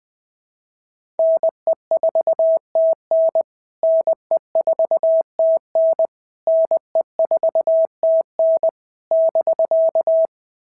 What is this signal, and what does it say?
Morse code, heard as a keyed tone, about 20 wpm, 650 Hz. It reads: NE4TN NE4TN NE4TN <BK>
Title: POTA QSO Practice - 20wpm